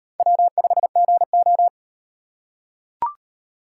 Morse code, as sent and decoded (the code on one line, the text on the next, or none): .-- ..... --. ---
W5GO